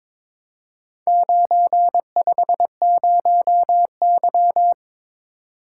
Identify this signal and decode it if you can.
Morse code, heard as a keyed tone, about 23 wpm, 700 Hz. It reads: 950Y
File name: practice / 22wpm / Four Letter-Number Combinations - 22wpm